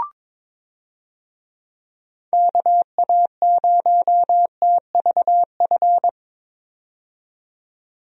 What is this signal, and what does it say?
Morse code, heard as a keyed tone, about 21 wpm, 700 Hz. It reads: KA0TVF
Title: USA Call Signs - 6 Characters - 22wpm